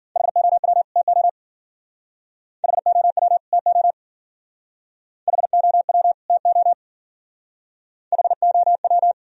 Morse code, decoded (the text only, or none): HOW TO HOW TO HOW TO HOW